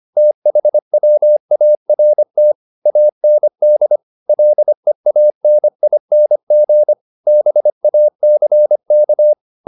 Morse code, heard as a keyed tone, about 25 wpm, 600 Hz. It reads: THWART AND LEANING BACK
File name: the old man and the sea part 1 25wpm